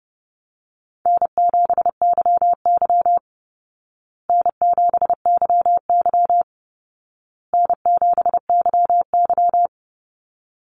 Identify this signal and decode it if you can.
Morse code, heard as a keyed tone, about 30 wpm, 700 Hz. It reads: N7YY N7YY N7YY
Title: Call Signs Repeated 2x - 30wpm